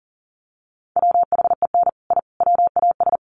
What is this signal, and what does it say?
Morse code, heard as a keyed tone, about 40 wpm, 700 Hz. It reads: WHEN I WAS
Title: Sets of 5 Words Encore 40wpm